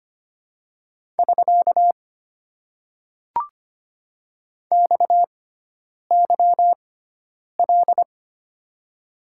Morse code, heard as a keyed tone, about 25 wpm, spaced as for 9 wpm, 700 Hz. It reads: <SK> XYL
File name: Top 90 QSO Elements - 6x Character Spacing - 25wpm